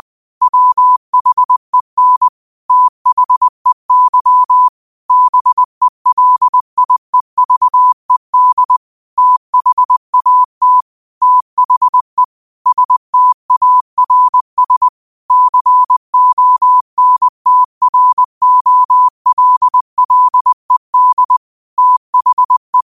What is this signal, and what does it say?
Morse code, heard as a keyed tone, about 20 wpm, 1000 Hz. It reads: WHEN THEY BELIEVED THAT THE STARS CONTROLLED THE
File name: qotdfiles / 20wpm